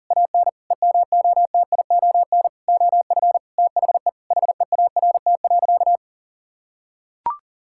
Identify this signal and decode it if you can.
Morse code, heard as a keyed tone, about 40 wpm, 700 Hz. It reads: AN EMOTION OF THE HEART.
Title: Sentences from Top 1600 Words 40wpm